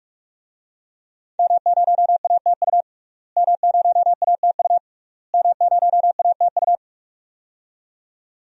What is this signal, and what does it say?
Morse code, heard as a keyed tone, about 46 wpm, 700 Hz. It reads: M0ATU M0ATU M0ATU
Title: DXCC-100 Call Signs - 5 Characters - Repeated 3x - 45wpm